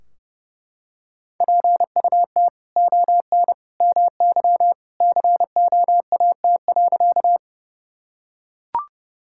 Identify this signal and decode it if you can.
Morse code, heard as a keyed tone, about 32 wpm, 700 Hz. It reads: PUT ON MY COAT.
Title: Sentences from Top 1600 Words 30wpm